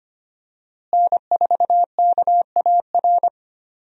Morse code, heard as a keyed tone, about 25 wpm, 700 Hz. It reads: N4KAR
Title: USA Call Signs - 5 Characters - Repeated 2x - 25wpm